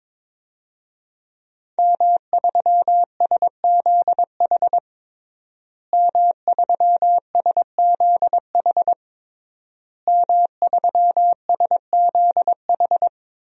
Morse code, heard as a keyed tone, about 23 wpm, 700 Hz. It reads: M3SZH M3SZH M3SZH
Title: DXCC-100 Call Signs - 5 Characters - Repeated 3x - 22wpm